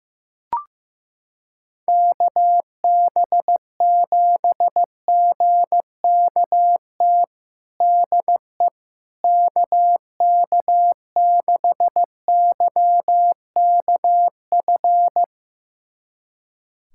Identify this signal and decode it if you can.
Morse code, heard as a keyed tone, about 15 wpm, 700 Hz. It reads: KB7GKT DE KK6YKF
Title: USA Callsign DE Callsign 15wpm